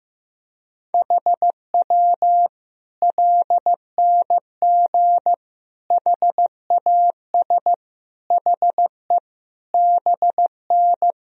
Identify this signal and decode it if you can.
Morse code, heard as a keyed tone, about 15 wpm, 700 Hz. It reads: HW LNG HAS HE BN
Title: Sentences from Top 100 Words and Abbreviations 15wpm